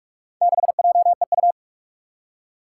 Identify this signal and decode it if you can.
Morse code, heard as a keyed tone, about 44 wpm, 700 Hz. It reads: 6JEU
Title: Four Challenging Letter-Number Combinations 45wpm